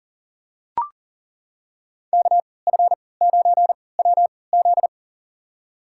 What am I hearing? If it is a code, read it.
Morse code, heard as a keyed tone, about 40 wpm, spaced as for 23 wpm, 700 Hz. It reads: KF9WZ
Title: USA Call Signs - 5 Characters - 3x Character Spacing - 40wpm